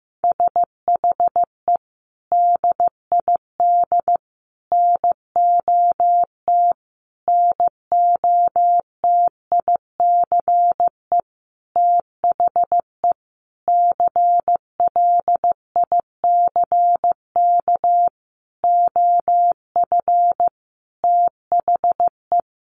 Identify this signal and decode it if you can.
Morse code, heard as a keyed tone, about 15 wpm, 700 Hz. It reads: SHE DID NOT NOTICE THE CLICK OF THE
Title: Sentences from Top 1100 Words 15wpm